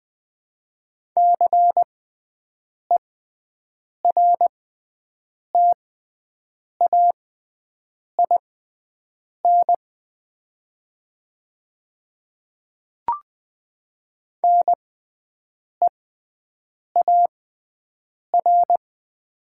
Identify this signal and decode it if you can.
Morse code, heard as a keyed tone, about 20 wpm, spaced as for 7 wpm, 700 Hz. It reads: CERTAIN NEAR
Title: 6x Character Spacing - Top 500 Words - 20wpm